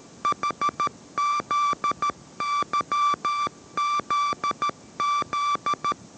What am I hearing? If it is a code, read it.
Morse code, heard as a keyed tone, about 13 wpm, 1170 Hz. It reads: HZYZZ